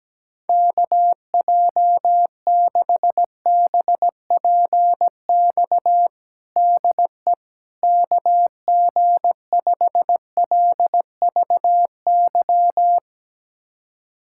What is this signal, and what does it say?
Morse code, heard as a keyed tone, about 17 wpm, 700 Hz. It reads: KJ6BPX DE KG5LVY
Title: USA Callsign DE Callsign - 6 Characters - 17wpm